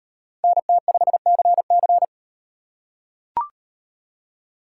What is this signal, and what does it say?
Morse code, heard as a keyed tone, about 36 wpm, 700 Hz. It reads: NT5CC